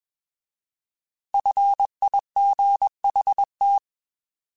Morse code, heard as a keyed tone, about 21 wpm, 800 Hz. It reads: FIGHT